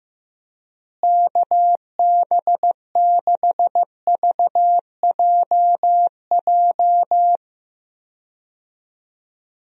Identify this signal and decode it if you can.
Morse code, heard as a keyed tone, about 15 wpm, 700 Hz. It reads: KB6VJJ
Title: USA Call Signs - 6 Characters - Repeated 3x - 15wpm